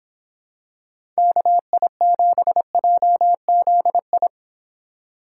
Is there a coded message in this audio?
Yes: Morse code from 1.2 to 4.3 s, heard as a keyed tone, about 26 wpm, 700 Hz. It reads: KI7JZI